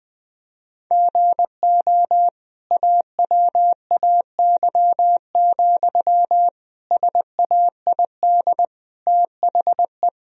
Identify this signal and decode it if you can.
Morse code, heard as a keyed tone, about 20 wpm, 700 Hz. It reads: GO AWAY, SAID THE
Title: Sentences from Top 1000 Words - Review Part 2 - 20wpm